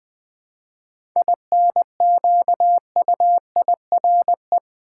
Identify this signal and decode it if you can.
Morse code, heard as a keyed tone, about 20 wpm, 700 Hz. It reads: INQUIRE
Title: 7 Letter Words - First 3 Letters Given - 20wpm